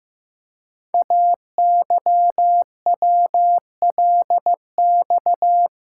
Morse code, heard as a keyed tone, about 15 wpm, 700 Hz. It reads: AYWLX
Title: Code Groups - 5 Letters - 15wpm